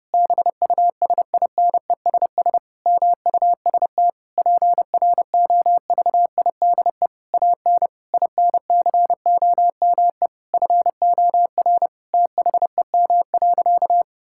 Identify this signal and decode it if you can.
Morse code, heard as a keyed tone, about 30 wpm, 700 Hz. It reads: BUSINESS MUST PROVIDE AN INCOME FOR THEM.
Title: Sentences from Top 1000 Words - Review Part 5 - 30wpm